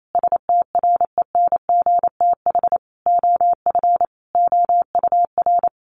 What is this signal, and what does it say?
Morse code, heard as a keyed tone, about 28 wpm, 700 Hz. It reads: STRENGTH OF OUR
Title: Sentences from Top 1300 Words 28wpm